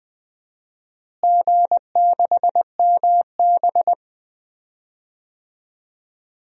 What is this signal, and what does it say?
Morse code, heard as a keyed tone, about 21 wpm, 700 Hz. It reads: G6MB
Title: DXCC-100 Call Signs - 4 Characters - Repeated 2x - 20wpm